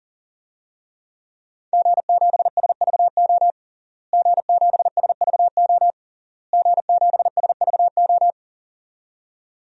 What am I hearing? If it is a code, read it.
Morse code, heard as a keyed tone, about 37 wpm, 700 Hz. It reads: G7SVO G7SVO G7SVO
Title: DXCC-100 Call Signs - 5 Characters - Repeated 3x - 40wpm